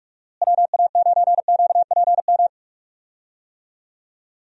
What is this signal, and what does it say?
Morse code, heard as a keyed tone, about 46 wpm, 700 Hz. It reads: WA9QPM
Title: USA Call Signs - 6 Characters - Repeated 3x - 45wpm